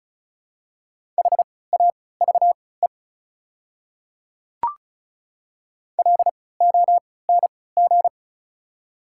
Morse code, heard as a keyed tone, about 36 wpm, spaced as for 20 wpm, 700 Hz. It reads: HAVE LONG
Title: 3x Character Spacing - Top 100 Words - 35wpm